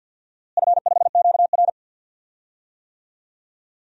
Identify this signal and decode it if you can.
Morse code, heard as a keyed tone, about 52 wpm, 700 Hz. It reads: F5QR
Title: DXCC-100 Call Signs - 4 Characters - Repeated 2x - 50wpm